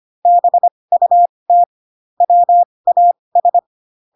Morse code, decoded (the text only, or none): BUT WAS